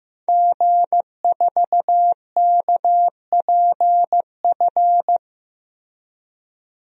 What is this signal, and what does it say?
Morse code, heard as a keyed tone, about 15 wpm, 700 Hz. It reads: G4KPF
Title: DXCC-100 Call Signs - 5 Characters - Repeated 3x - 15wpm